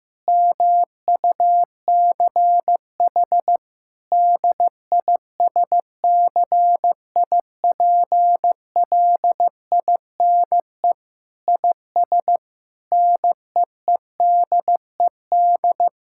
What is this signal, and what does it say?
Morse code, heard as a keyed tone, about 15 wpm, 700 Hz. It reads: MUCH DISCIPLINE IS NEEDED
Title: Sentences from Top 1000 Words - Review Part 6 - 15wpm